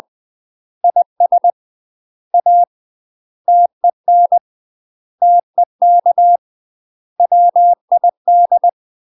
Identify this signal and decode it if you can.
Morse code, heard as a keyed tone, about 20 wpm, 700 Hz. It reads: IS A TEN TEK WID
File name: UK MASTER QSOs TRAINING / 20WPM / UK MASTER QSOs 0042